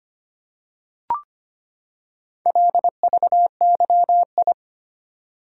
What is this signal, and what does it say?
Morse code, heard as a keyed tone, about 25 wpm, 700 Hz. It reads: LVYI
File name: practice / 25wpm / Four Letter-Number Combinations - 25wpm